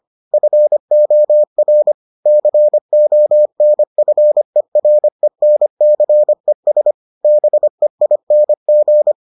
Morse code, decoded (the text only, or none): FOR CONFERENCES BEING